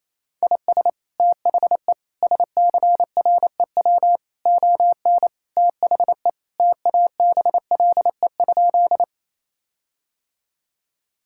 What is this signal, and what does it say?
Morse code, heard as a keyed tone, about 28 wpm, 700 Hz. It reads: IS THE SCREW ON THE TABLE?
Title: Sentences from Top 1600 Words 28wpm